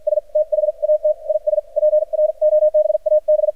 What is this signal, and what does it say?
Morse code, heard as a keyed tone, about 51 wpm, spaced as for 44 wpm, 610 Hz, with radio noise. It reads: S THAT IS PROBAB
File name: archived / CRSnet